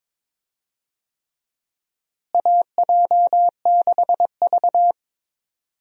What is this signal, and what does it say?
Morse code, heard as a keyed tone, about 22 wpm, 700 Hz. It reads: AJ6V
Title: USA Call Signs - 4 Characters - Repeated 3x - 22wpm